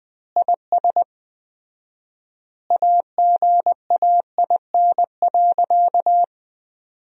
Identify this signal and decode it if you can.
Morse code, heard as a keyed tone, about 21 wpm, 700 Hz. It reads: IS AGAIN.
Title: 4x Word-Spacing - Sentences from Top 200 Words - 20wpm